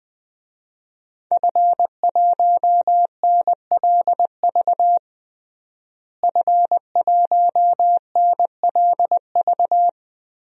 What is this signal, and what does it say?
Morse code, heard as a keyed tone, about 20 wpm, 700 Hz. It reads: F1NLV F1NLV
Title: DXCC-100 Call Signs - 5 Characters - Repeated 3x - 20wpm